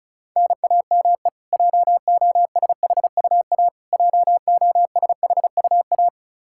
Morse code, heard as a keyed tone, about 36 wpm, 700 Hz. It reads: NAME JOSHUA JOSHUA
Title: Practice QSO - Exchange 1 - Beginning - 35wpm